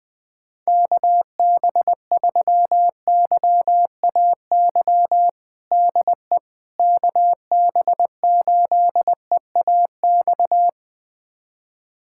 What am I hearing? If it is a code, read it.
Morse code, heard as a keyed tone, about 20 wpm, 700 Hz. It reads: KB3YAY DE KB8EAX
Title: USA Callsign DE Callsign - 6 Characters - 20wpm